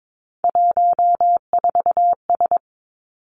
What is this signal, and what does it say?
Morse code, heard as a keyed tone, about 22 wpm, 700 Hz. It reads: 14S
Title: Three Challenging Letter-Number Combinations 22wpm